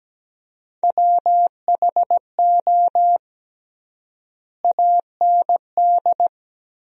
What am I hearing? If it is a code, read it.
Morse code, heard as a keyed tone, about 17 wpm, 700 Hz. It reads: WHO AND
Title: 3x Word-Spacing - Sentences from Top 100 Words - 17wpm